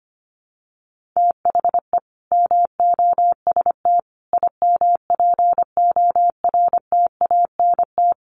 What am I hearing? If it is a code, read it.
Morse code, heard as a keyed tone, about 25 wpm, 700 Hz. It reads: THE MOST IMPORTANT
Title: Sentences from Top 1400 Words 25wpm